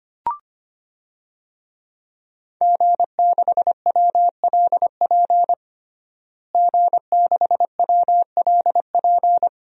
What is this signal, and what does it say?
Morse code, heard as a keyed tone, about 25 wpm, 700 Hz. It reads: G6WLP G6WLP
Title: DXCC-100 Call Signs - 5 Characters - Repeated 2x - 25wpm